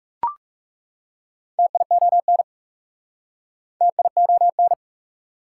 Morse code, heard as a keyed tone, about 37 wpm, 690 Hz. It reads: TION TION
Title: Speed-Racing Prefixes and Suffixes 30wpm